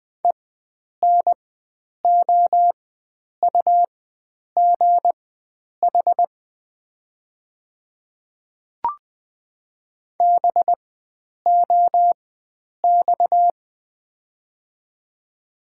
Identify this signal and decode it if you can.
Morse code, heard as a keyed tone, about 20 wpm, spaced as for 9 wpm, 700 Hz. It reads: ENOUGH BOX
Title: 4x Character Spacing - Top 500 Words - 20wpm